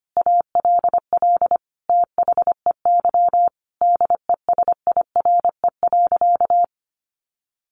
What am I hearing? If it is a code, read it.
Morse code, heard as a keyed tone, about 25 wpm, 700 Hz. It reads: ALL THEY DESIRE.
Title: Sentences from Top 1100 Words 25wpm